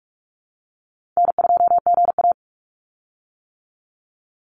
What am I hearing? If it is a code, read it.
Morse code, heard as a keyed tone, about 44 wpm, 700 Hz. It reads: N2GA
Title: Easy Call Signs - 45wpm